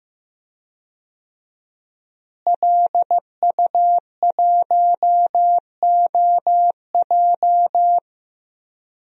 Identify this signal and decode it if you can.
Morse code, heard as a keyed tone, about 15 wpm, 700 Hz. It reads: LU1OJ
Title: DXCC-100 Call Signs - 5 Characters - Repeated 2x - 15wpm